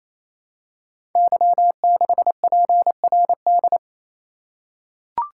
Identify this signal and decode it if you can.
Morse code, heard as a keyed tone, about 27 wpm, 700 Hz. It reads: Y6PRD
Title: Code Groups - 5 Letter:Numbers - 28wpm